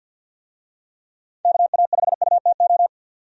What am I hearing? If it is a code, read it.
Morse code, heard as a keyed tone, about 50 wpm, 700 Hz. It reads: KA5UTO
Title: USA Call Signs - 6 Characters - 50wpm